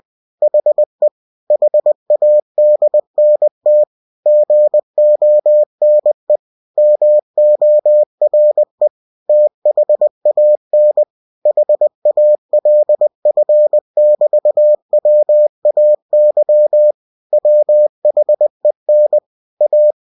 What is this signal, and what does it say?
Morse code, heard as a keyed tone, about 20 wpm, 600 Hz. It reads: HE HADNT GONE MORE THAN HALF<BT>WAY WHEN A
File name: cw-books / winnie-the-pooh / winnie-20wpm-page-0049